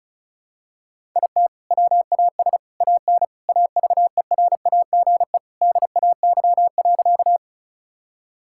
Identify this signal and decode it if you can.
Morse code, heard as a keyed tone, about 36 wpm, 700 Hz. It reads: IT WAS AN AVERAGE DAY.
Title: Sentences from Top 1000 Words - Review Part 6 - 35wpm